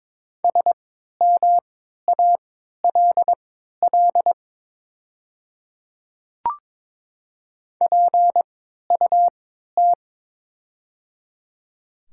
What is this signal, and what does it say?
Morse code, heard as a keyed tone, about 22 wpm, spaced as for 12 wpm, 700 Hz. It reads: SMALL PUT
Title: 3x Character Spacing - Top 200 Words - 22wpm